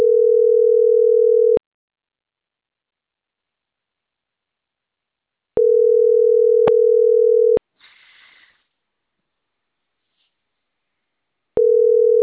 Ringback tone (North America), 440 + 480 Hz.